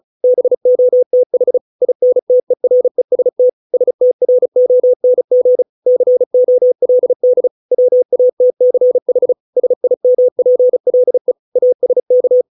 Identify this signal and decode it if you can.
Morse code, heard as a keyed tone, about 34 wpm, 500 Hz. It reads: BOTH INTEREST STRONG COLD WATCH SIMPLE ASK